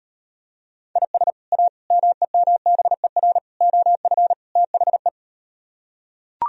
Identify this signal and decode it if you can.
Morse code, heard as a keyed tone, about 38 wpm, 700 Hz. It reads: IS A MEMBER OF THE